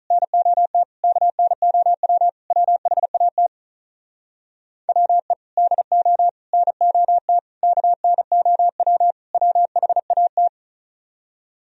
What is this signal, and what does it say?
Morse code, heard as a keyed tone, about 47 wpm, 700 Hz. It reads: NOT KNOW WHAT WE DO NOT KNOW WHAT